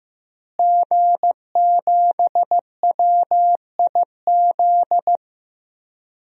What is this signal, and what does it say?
Morse code, heard as a keyed tone, about 16 wpm, 700 Hz. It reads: G7WIZ
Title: DXCC-100 Call Signs - 5 Characters - Repeated 3x - 15wpm